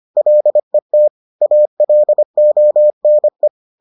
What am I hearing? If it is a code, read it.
Morse code, heard as a keyed tone, about 25 wpm, 600 Hz. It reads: LET ALONE